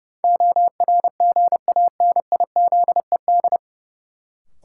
Morse code, heard as a keyed tone, about 30 wpm, 700 Hz. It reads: ORGANIZED
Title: Sentences from Top 1300 Words - New Word Drill - 30wpm